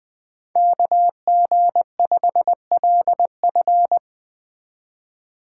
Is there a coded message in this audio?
Yes: Morse code from 0.6 to 4.0 s, heard as a keyed tone, about 19 wpm, 700 Hz. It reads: KG5LF